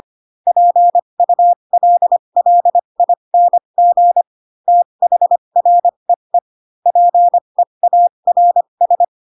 Morse code, heard as a keyed tone, about 25 wpm, spaced as for 22 wpm, 700 Hz. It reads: PULLING THREE PEARS
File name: Pinocchio / Pinocchio ch7 18 25 ik1hge